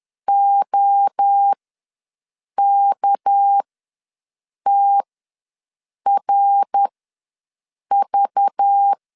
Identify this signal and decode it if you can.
Morse code, heard as a keyed tone, about 10 wpm, spaced as for 6 wpm, 800 Hz. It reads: OKTRV
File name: LEZIONI / 6 PKX0 / 6 PK ws 10wpm 3745